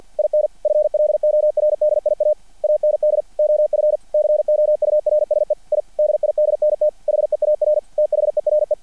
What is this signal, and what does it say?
Morse code, heard as a keyed tone, about 50 wpm, 600 Hz. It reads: IN FLORDIA AND OF COURSE I DIDNT HEAR THERE
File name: archived / CRSnet